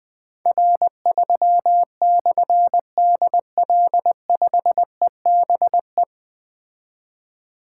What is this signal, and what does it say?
Morse code, heard as a keyed tone, about 20 wpm, 700 Hz. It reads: R3/DL5EBE
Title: Difficult Call Signs 20wpm